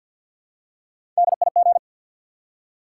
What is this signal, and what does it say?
Morse code, heard as a keyed tone, about 53 wpm, 700 Hz. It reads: DIG
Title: Three Letter Words - 50wpm